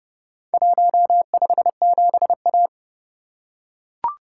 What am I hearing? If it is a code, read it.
Morse code, heard as a keyed tone, about 27 wpm, 700 Hz. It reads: 157A